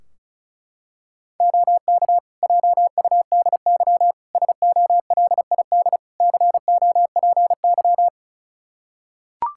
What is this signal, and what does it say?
Morse code, heard as a keyed tone, about 34 wpm, 700 Hz. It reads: OK JUDY SOLID COPY
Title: Practice QSO - Exchange 3 - More Info - 35wpm